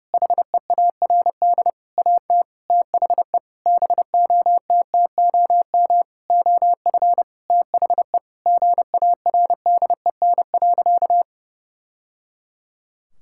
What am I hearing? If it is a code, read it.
Morse code, heard as a keyed tone, about 32 wpm, 700 Hz. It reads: HEARD AT THE BOTTOM OF THE GARDEN.
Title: Sentences from Top 1000 Words - Review Part 6 - 30wpm